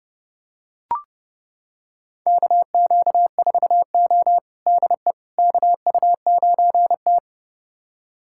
Morse code, heard as a keyed tone, about 32 wpm, 700 Hz. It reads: KQ4O DE KU9T
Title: USA Callsign DE Callsign - 4 Characters - 30wpm